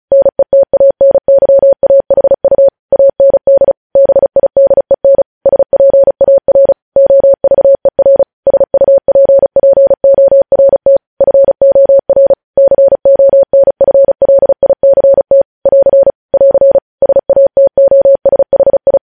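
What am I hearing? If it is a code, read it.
Morse code, heard as a keyed tone, about 36 wpm, 550 Hz. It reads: NETANYAHU AND BIDEN SPAR OVER SUPPORT FOR CONFLICT <AR> <AR> SATOSHI
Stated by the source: Morse code news headlines at 35.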